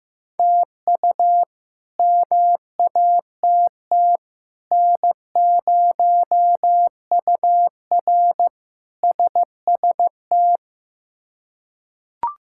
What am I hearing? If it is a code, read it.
Morse code, heard as a keyed tone, about 15 wpm, 700 Hz. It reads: TU MATT N0UR SST
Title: K1USN SST QSO Practice - 15wpm